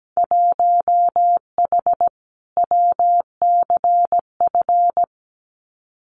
Morse code, heard as a keyed tone, about 17 wpm, 700 Hz. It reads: 1H WCF
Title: Winter Field Day Exchanges 17wpm